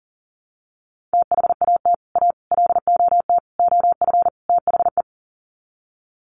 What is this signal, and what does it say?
Morse code, heard as a keyed tone, about 40 wpm, 700 Hz. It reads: THAT A LOT OF THE
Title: Sets of 5 Words Encore2 40wpm